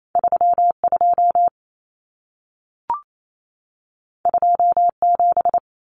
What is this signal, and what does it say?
Morse code, heard as a keyed tone, about 28 wpm, 700 Hz. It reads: 32 27
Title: Common Cut Numbers - 2-Digit 28wpm